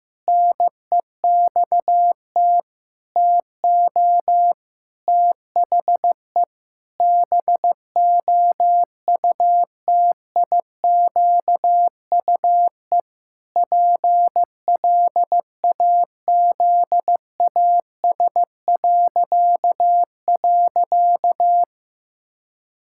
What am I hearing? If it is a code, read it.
Morse code, heard as a keyed tone, about 16 wpm, 700 Hz. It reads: NEXT TO THE BOUTIQUE PLAZAS..
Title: Pangram Challenge v2 - 15wpm